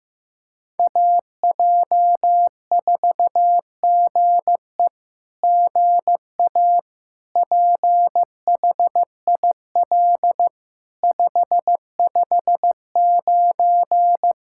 Morse code, heard as a keyed tone, about 15 wpm, 700 Hz. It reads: AJ4GE GA PHIL 559
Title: SOTA QSO Practice 15wpm